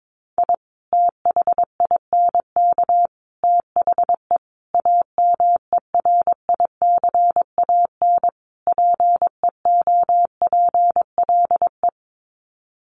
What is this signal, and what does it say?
Morse code, heard as a keyed tone, about 22 wpm, 700 Hz. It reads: I THINK THE AMERICAN PEOPLE
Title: Sets of 5 Words Encore 22wpm